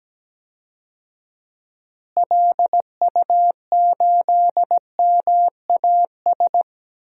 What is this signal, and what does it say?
Morse code, heard as a keyed tone, about 17 wpm, 700 Hz. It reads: LU8MAS
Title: DXCC-100 Call Signs - 6 Characters - Repeated 2x - 17wpm